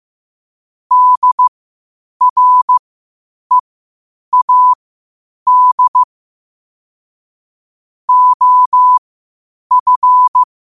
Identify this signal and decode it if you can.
Morse code, heard as a keyed tone, about 15 wpm, spaced as for 8 wpm, 1000 Hz. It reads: DREAD OF
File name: qotdfiles / 8wpm